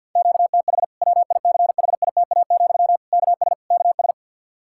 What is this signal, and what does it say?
Morse code, heard as a keyed tone, about 49 wpm, 700 Hz. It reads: QTH WICHITA, KS KS
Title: Practice QSO - Exchange 1 - Beginning - 50wpm